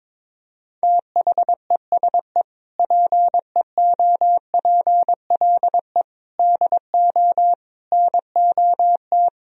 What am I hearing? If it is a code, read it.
Morse code, heard as a keyed tone, about 21 wpm, 700 Hz. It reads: THESE PEOPLE DO NOT
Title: Sentences from Top 1000 Words - Review Part 1 - 22wpm